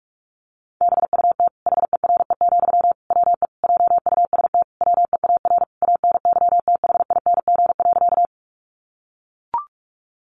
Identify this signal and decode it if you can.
Morse code, heard as a keyed tone, about 44 wpm, 700 Hz. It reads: BUT HERE, WE JUST WEAR ANYTHING.
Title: Sentences from Top 1500 Words 45wpm